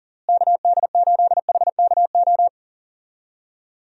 Morse code, heard as a keyed tone, about 40 wpm, 700 Hz. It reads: KD8HKO
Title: USA Call Signs - 6 Characters - Repeated 3x - 40wpm